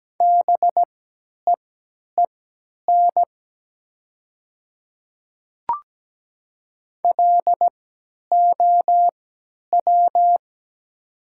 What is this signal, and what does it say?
Morse code, heard as a keyed tone, about 17 wpm, spaced as for 10 wpm, 700 Hz. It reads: BEEN LOW
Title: 3x Character Spacing - Top 200 Words - 17wpm